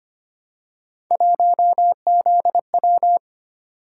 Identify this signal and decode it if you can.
Morse code, heard as a keyed tone, about 25 wpm, 700 Hz. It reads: 1ZW
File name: Three Letter-Number Combinations - 25wpm